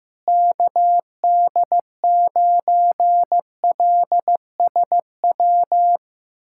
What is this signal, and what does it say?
Morse code, heard as a keyed tone, about 15 wpm, 700 Hz. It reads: KD9LSW
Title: USA Callsign DE Callsign 15wpm